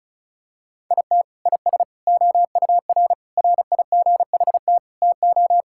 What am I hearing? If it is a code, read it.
Morse code, heard as a keyed tone, about 34 wpm, 700 Hz. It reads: IT IS OUR RIGHT TO
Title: Sentences from Top 1000 Words - Review Part 5 - 35wpm